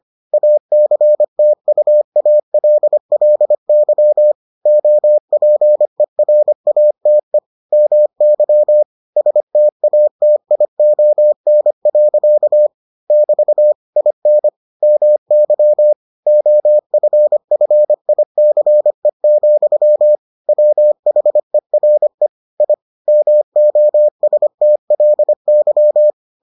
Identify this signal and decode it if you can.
Morse code, heard as a keyed tone, about 25 wpm, 600 Hz. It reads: ACTUALLY OPERATE MY STATION. <BT> IN MY OFFICE, WHERE I MOSTLY